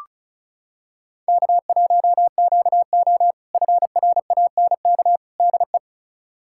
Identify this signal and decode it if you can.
Morse code, heard as a keyed tone, about 36 wpm, 700 Hz. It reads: K1QO FRANK DE
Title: K1USN SST QSO Practice - 35wpm